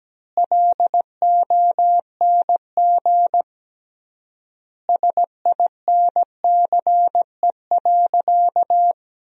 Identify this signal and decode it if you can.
Morse code, heard as a keyed tone, about 17 wpm, 700 Hz. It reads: LONG SINCE.
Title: 3x Word-Spacing - Sentences from Top 300 Words - 17wpm